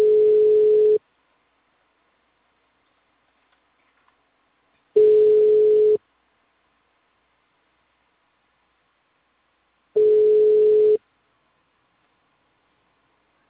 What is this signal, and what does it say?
Ringback tone (Europe, 425 Hz).